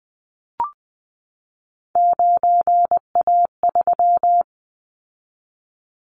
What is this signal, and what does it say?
Morse code, heard as a keyed tone, about 20 wpm, 700 Hz. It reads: 9A3
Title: Callsign Prefixes for Top 100 DXCC - 20wpm